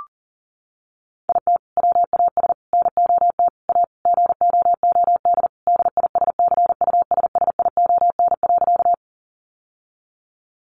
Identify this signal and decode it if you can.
Morse code, heard as a keyed tone, about 40 wpm, 700 Hz. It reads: IT WAS NOT A GOOD DISCUSSION.
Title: Sentences from Top 1100 Words 40wpm